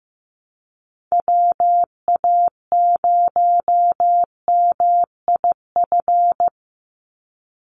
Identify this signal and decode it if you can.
Morse code, heard as a keyed tone, about 15 wpm, 700 Hz. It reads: WA0MIF
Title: USA Call Signs 15wpm